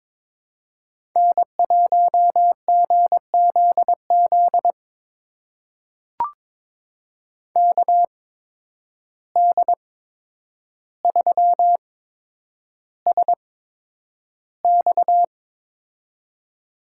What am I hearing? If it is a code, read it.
Morse code, heard as a keyed tone, about 21 wpm, spaced as for 9 wpm, 700 Hz. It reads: N1GZZ KD3SX
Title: USA Call Signs - 5 Characters - 8x Character Spacing - 22wpm